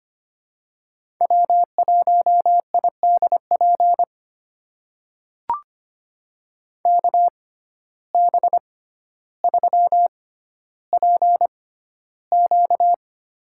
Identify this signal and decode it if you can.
Morse code, heard as a keyed tone, about 25 wpm, spaced as for 13 wpm, 700 Hz. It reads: W1IDP KB3PQ
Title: USA Call Signs - 6x Character Spacing - 25wpm